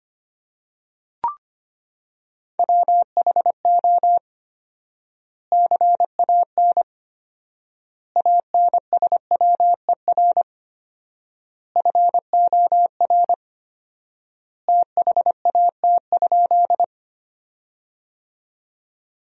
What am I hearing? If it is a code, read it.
Morse code, heard as a keyed tone, about 25 wpm, 700 Hz. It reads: WHO CAN ANSWER FOR THAT?
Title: 4x Word-Spacing - Sentences from Top 300 Words - 25wpm